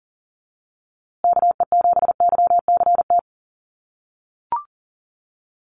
Morse code, heard as a keyed tone, about 40 wpm, 700 Hz. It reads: KE7YCT
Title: USA Call Signs Repeated 2x 40wpm